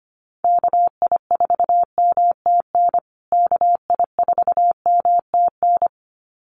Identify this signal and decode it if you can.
Morse code, heard as a keyed tone, about 25 wpm, 700 Hz. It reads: KI4MTN KI4MTN
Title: USA Call Signs Repeated 2x 25wpm